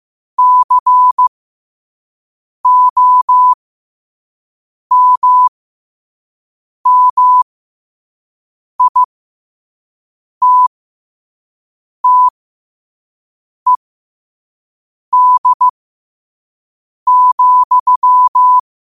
Morse code, heard as a keyed tone, about 15 wpm, spaced as for 5 wpm, 1000 Hz. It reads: COMMITTED,
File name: qotdfiles / 5wpm